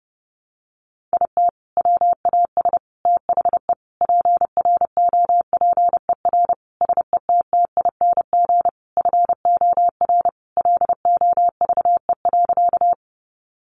Morse code, heard as a keyed tone, about 30 wpm, 700 Hz. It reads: IT WAS THE PROPER SETTING FOR LOVE.
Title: Sentences from Top 1300 Words 30wpm